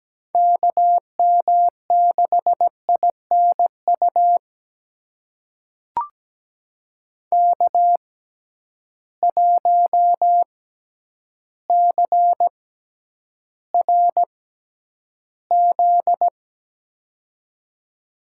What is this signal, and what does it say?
Morse code, heard as a keyed tone, about 17 wpm, spaced as for 9 wpm, 700 Hz. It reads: KM6INU K1CRZ
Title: USA Call Signs - 6x Character Spacing - 17wpm